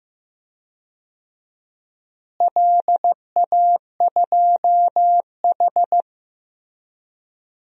Morse code, heard as a keyed tone, about 15 wpm, 700 Hz. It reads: LA2H
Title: DXCC-100 Call Signs - 4 Characters - Repeated 2x - 15wpm